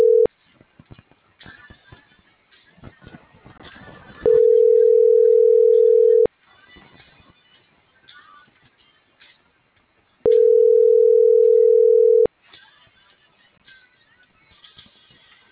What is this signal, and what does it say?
Ringback tone (North America), 440 + 480 Hz.